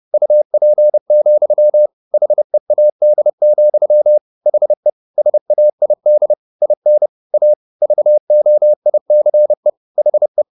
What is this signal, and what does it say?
Morse code, heard as a keyed tone, about 30 wpm, 600 Hz. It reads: UP, HEAD, HE SAID IN A VOICE HE